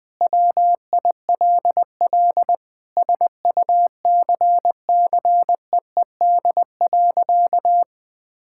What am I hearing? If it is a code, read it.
Morse code, heard as a keyed tone, about 20 wpm, 700 Hz. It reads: WILL SUCCEED.
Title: Sentences with 8-Word Phrases - 20wpm